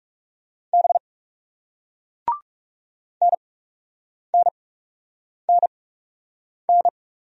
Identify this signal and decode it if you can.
Morse code, heard as a keyed tone, about 38 wpm, spaced as for 8 wpm, 700 Hz. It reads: B NNNN